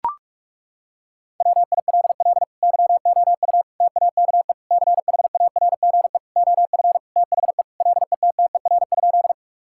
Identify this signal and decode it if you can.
Morse code, heard as a keyed tone, about 41 wpm, 700 Hz. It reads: WILL YOU TAKE CHARGE OF THE LETTER?
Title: Sentences from Top 1000 Words - Review Part 6 - 45wpm